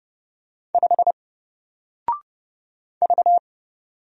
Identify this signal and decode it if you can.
Morse code, heard as a keyed tone, about 30 wpm, 700 Hz. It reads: 5 V
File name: Single-Letter-Number-30wpm